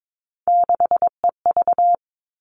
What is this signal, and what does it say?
Morse code, heard as a keyed tone, about 22 wpm, 700 Hz. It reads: 6EV
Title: All Cut Numbers - 3-Digit 22wpm